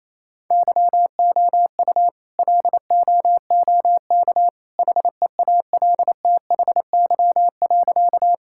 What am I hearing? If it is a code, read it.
Morse code, heard as a keyed tone, about 27 wpm, 700 Hz. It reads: YOU LOOK HEALTHY.
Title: Sentences from Top 1000 Words - Review Part 5 - 28wpm